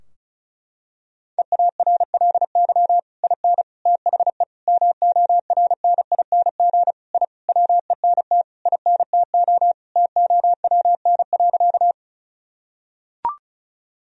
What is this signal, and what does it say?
Morse code, heard as a keyed tone, about 36 wpm, 700 Hz. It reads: EARLY IN THE MORNING I WENT INTO TOWN.
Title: Sentences from Top 1000 Words - Review Part 2 - 35wpm